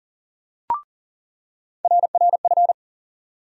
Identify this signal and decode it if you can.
Morse code, heard as a keyed tone, about 40 wpm, 700 Hz. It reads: RRF
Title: USA Call Sign Postfixes - 3 Letters 40wpm